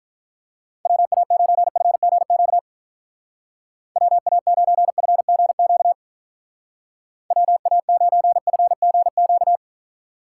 Morse code, heard as a keyed tone, about 44 wpm, 700 Hz. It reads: WA9FGQ WA9FGQ WA9FGQ
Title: Speed Racing - USA 2x3 Call Signs 35wpm